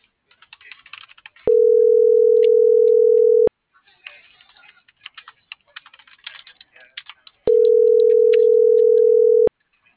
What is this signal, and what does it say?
Ringback tone (North America), 440 + 480 Hz.